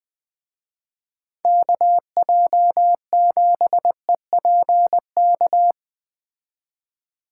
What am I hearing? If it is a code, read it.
Morse code, heard as a keyed tone, about 19 wpm, 700 Hz. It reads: KJ7EPK
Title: Speed Racing - USA 2x3 Call Signs 17wpm